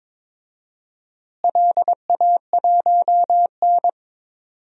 Morse code, heard as a keyed tone, about 22 wpm, 700 Hz. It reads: LA1N
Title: DXCC-100 Call Signs - 4 Characters - Repeated 2x - 22wpm